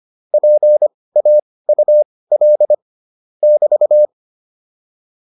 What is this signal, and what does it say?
Morse code, heard as a keyed tone, about 25 wpm, spaced as for 18 wpm, 600 Hz. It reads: PAUL <BT>
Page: intermediate-practice-files